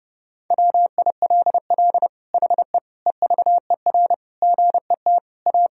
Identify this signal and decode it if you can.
Morse code, heard as a keyed tone, about 30 wpm, 700 Hz. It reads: WILL HE EVER GET A
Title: Sentences from Top 1000 Words - Review Part 3 - 30wpm